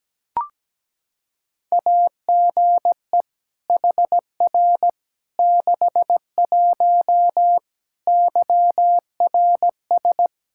Morse code, heard as a keyed tone, about 17 wpm, 700 Hz. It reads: AGE HR 61 YRS
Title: Practice QSO - Exchange 3 - More Info - 17wpm